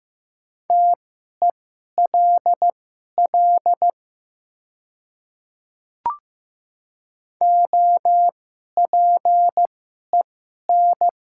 Morse code, heard as a keyed tone, about 15 wpm, spaced as for 11 wpm, 700 Hz. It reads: TELL OPEN
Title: 2x Character Spacing - Top 500 Words - 15wpm